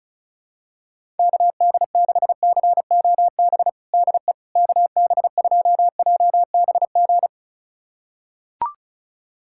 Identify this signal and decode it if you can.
Morse code, heard as a keyed tone, about 34 wpm, 700 Hz. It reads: KD6COB DE KB2JBG
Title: USA Callsign DE Callsign - 6 Characters - 35wpm